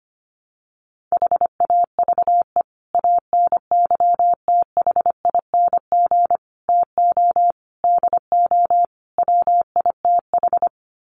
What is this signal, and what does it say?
Morse code, heard as a keyed tone, about 25 wpm, 700 Hz. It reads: HAVE ANYTHING TO DO WITH
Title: Sets of 5 Words 25wpm